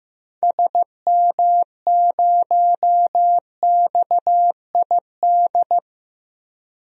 Morse code, heard as a keyed tone, about 15 wpm, 700 Hz. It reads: SM0XID
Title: DXCC-100 Call Signs - 6 Characters - Repeated 2x - 15wpm